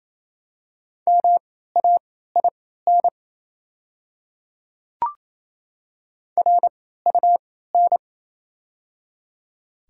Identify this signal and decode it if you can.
Morse code, heard as a keyed tone, about 28 wpm, spaced as for 16 wpm, 700 Hz. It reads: MAIN RUN
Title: 3x Character Spacing - Top 300 Words - 28wpm